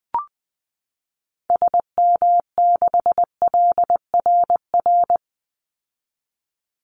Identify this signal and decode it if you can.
Morse code, heard as a keyed tone, about 20 wpm, 700 Hz. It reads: SM6LRR
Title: Call Signs Repeated 3x - 20wpm